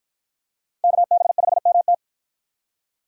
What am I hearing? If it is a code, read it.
Morse code, heard as a keyed tone, about 53 wpm, 700 Hz. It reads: KB5MT